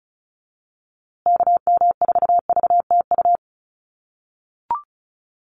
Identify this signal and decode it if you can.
Morse code, heard as a keyed tone, about 34 wpm, 700 Hz. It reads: KM4VTU
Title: USA Call Signs Repeated 2x 35wpm